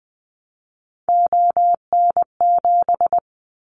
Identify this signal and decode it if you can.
Morse code, heard as a keyed tone, about 20 wpm, 700 Hz. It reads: ON7
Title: Callsign Prefixes for Top 25 DXCC - Repeated 3x - 20wpm